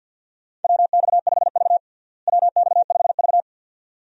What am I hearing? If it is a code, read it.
Morse code, heard as a keyed tone, about 48 wpm, 700 Hz. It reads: WX5V WX5V
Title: USA Call Signs - 4 Characters - Repeated 2x - 50wpm